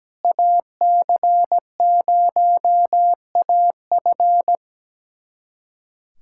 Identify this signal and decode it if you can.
Morse code, heard as a keyed tone, about 17 wpm, 700 Hz. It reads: AC0AF
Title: USA Callsign DE Callsign - 5 Characters - 17wpm